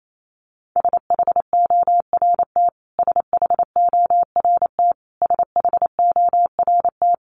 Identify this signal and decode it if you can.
Morse code, heard as a keyed tone, about 28 wpm, 700 Hz. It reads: SHORT SHORT SHORT
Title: Top 500 Words Repeated 28wpm